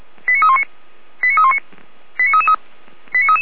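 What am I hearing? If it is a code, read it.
Five-tone selective call (ZVEI), heard as quick runs of five tones, tones of about 70 ms. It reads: 89319 89319
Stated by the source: ZVEI five-tone selective call 89500 received on an SDR by the Feuerwehr Orlamünde fire brigade server.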